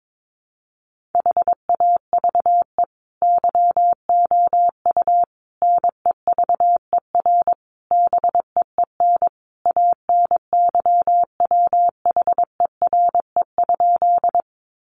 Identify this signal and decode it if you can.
Morse code, heard as a keyed tone, about 22 wpm, 700 Hz. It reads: HAVE YOU NEVER BEEN ANYWHERE?
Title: Sentences from Top 1400 Words 22wpm